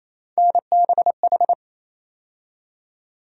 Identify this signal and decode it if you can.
Morse code, heard as a keyed tone, about 28 wpm, 700 Hz. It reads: NBH
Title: USA Call Sign Postfixes - 3 Letters 28wpm